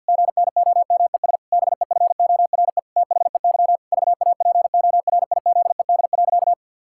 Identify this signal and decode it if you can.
Morse code, heard as a keyed tone, about 50 wpm, 700 Hz. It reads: GNOMES BEFORE THEY VAPORIZED.
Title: Pangram Challenge v2 - 50wpm